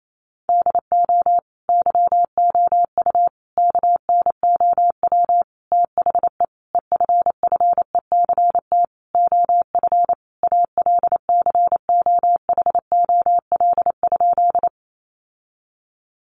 Morse code, heard as a keyed tone, about 28 wpm, 700 Hz. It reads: DO YOU KNOW THE EFFECT OF ALCOHOL?
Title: Sentences from Top 1400 Words 28wpm